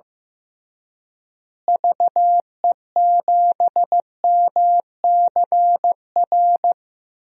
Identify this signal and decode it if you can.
Morse code, heard as a keyed tone, about 15 wpm, 700 Hz. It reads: VE7MCR
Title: DXCC-100 Call Signs - 6 Characters - Repeated 3x - 15wpm